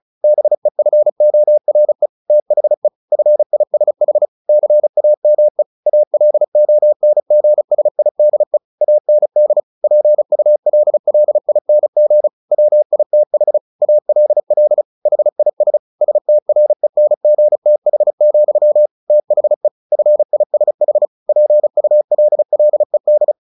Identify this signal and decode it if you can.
Morse code, heard as a keyed tone, about 35 wpm, 600 Hz. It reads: BEFORE THE FISH CAME ALONGSIDE AND PULLING WITH ALL HIS STRENGTH, THE FISH PULLED
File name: the old man and the sea 2 35wpm